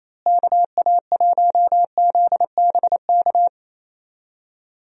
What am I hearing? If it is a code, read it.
Morse code, heard as a keyed tone, about 27 wpm, 700 Hz. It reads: KA1ZBK
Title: USA Call Signs - 6 Characters - Repeated 3x - 28wpm